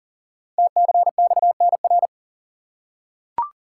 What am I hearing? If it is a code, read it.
Morse code, heard as a keyed tone, about 38 wpm, 690 Hz. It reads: TCXNR
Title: Code Groups - 5 Letters - 40wpm